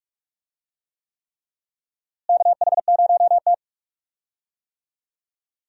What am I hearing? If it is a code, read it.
Morse code, heard as a keyed tone, about 46 wpm, 700 Hz. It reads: KH0T
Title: USA Call Signs - 4 Characters - Repeated 3x - 45wpm